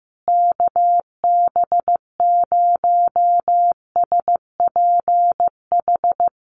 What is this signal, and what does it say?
Morse code, heard as a keyed tone, about 15 wpm, 700 Hz. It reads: KB0SPH
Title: USA Call Signs Repeated 2x 15wpm